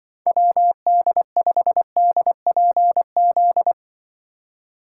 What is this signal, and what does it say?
Morse code, heard as a keyed tone, about 24 wpm, 700 Hz. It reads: WD5DPZ
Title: Speed Racing - USA 2x3 Call Signs 20wpm